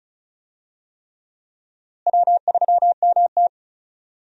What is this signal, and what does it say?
Morse code, heard as a keyed tone, about 34 wpm, 700 Hz. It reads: W3MT
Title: USA Call Signs - 4 Characters - Repeated 3x - 35wpm